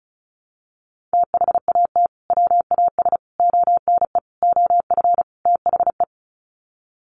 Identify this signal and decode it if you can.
Morse code, heard as a keyed tone, about 34 wpm, 700 Hz. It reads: THAT WAS ONE OF THE
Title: Sets of 5 Words 35wpm